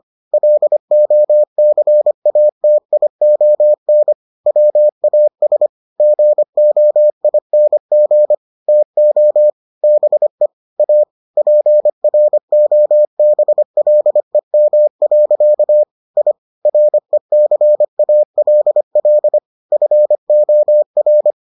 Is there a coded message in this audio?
Yes: Morse code from 0.3 to 21.3 s, heard as a keyed tone, about 25 wpm, 600 Hz. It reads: LOCATION WAS GOING TO BE A PROBLEM. I RECALL FOR